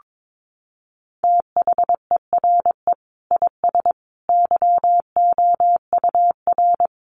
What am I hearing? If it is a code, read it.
Morse code, heard as a keyed tone, about 22 wpm, 700 Hz. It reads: THERE IS YOUR
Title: Sentences from Top 800 Words 22wpm